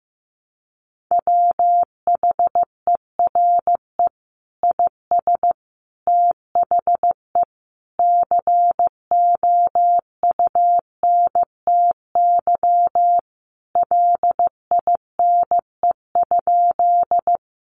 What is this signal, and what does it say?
Morse code, heard as a keyed tone, about 15 wpm, 700 Hz. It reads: WHERE IS THE COUNTY LINE?
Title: Sentences from Top 1200 Words 15wpm